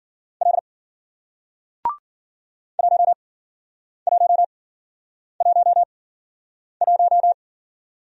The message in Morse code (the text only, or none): L 1111